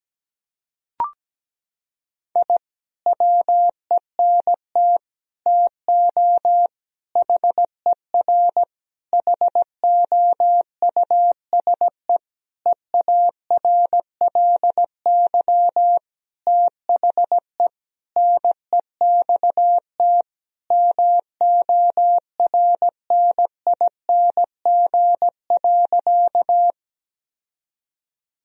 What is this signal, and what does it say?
Morse code, heard as a keyed tone, about 17 wpm, 700 Hz. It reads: I WENT TO HER HOUSE EARLY THE NEXT MORNING.
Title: Sentences from Top 1000 Words - Review Part 2 - 17wpm